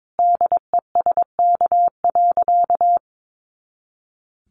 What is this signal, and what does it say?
Morse code, heard as a keyed tone, about 22 wpm, 700 Hz. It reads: DESK.
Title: A Sound Of Thunder 22wpm